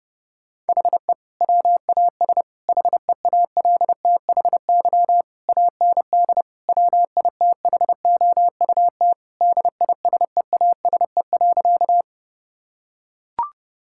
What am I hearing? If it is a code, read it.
Morse code, heard as a keyed tone, about 30 wpm, 700 Hz. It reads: HE WAS HEALTHY AND WITHOUT DISEASE.
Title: Sentences from Top 1000 Words - Review Part 6 - 30wpm